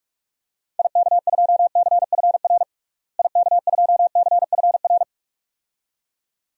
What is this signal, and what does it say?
Morse code, heard as a keyed tone, about 46 wpm, 700 Hz. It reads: IK2CFR IK2CFR
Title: DXCC-100 Call Signs - 6 Characters - Repeated 2x - 45wpm